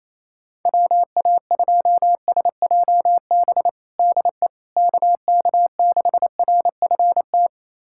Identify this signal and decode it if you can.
Morse code, heard as a keyed tone, about 28 wpm, 700 Hz. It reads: WA2SJB DE KK6RFT
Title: USA Callsign DE Callsign - 6 Characters - 28wpm